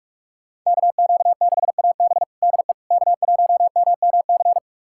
Morse code, heard as a keyed tone, about 47 wpm, 700 Hz. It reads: KQ6AB DE K1MMC
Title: USA Callsign DE Callsign - 5 Characters - 45wpm